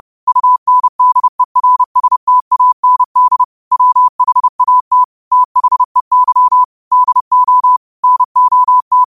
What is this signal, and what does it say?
Morse code, heard as a keyed tone, about 30 wpm, 1000 Hz. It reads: UNDERSTAND WHAT THEY DO NOT